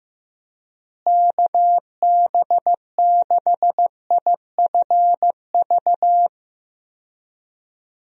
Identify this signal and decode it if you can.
Morse code, heard as a keyed tone, about 15 wpm, 700 Hz. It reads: KB6IFV
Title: USA Call Signs - 6 Characters - Repeated 3x - 15wpm